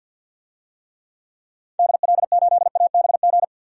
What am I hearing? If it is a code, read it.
Morse code, heard as a keyed tone, about 49 wpm, 700 Hz. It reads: DL8ABG
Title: DXCC-100 Call Signs - 6 Characters - 50wpm